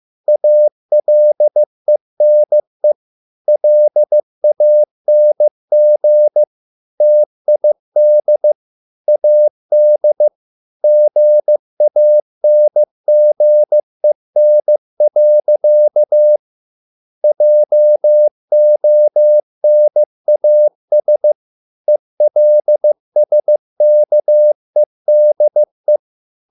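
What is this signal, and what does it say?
Morse code, heard as a keyed tone, about 15 wpm, 600 Hz. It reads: ALENE LANG TID AD GANGEN. JONAS ELSKEDE
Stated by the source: Morse code at a slow 15 wpm from CW med Gnister.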